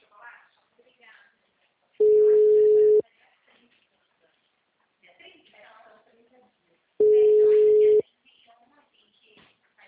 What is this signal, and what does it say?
Ringback tone (Europe, 425 Hz), with speech.
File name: out- -2004- - - .80